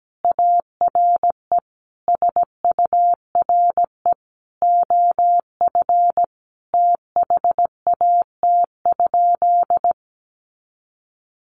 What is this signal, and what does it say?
Morse code, heard as a keyed tone, about 17 wpm, 700 Hz. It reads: ARE SURE OF THAT?
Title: Sentences from Top 300 Words 17wpm